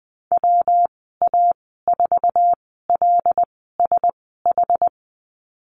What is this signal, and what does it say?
Morse code, heard as a keyed tone, about 20 wpm, spaced as for 14 wpm, 700 Hz. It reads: WA4LSH
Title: 179 Lesson 29g - Slash - Call Signs 20wpm